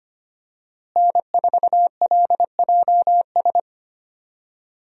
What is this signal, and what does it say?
Morse code, heard as a keyed tone, about 25 wpm, 700 Hz. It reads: N4LJS
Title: USA Call Signs - 5 Characters - Repeated 3x - 25wpm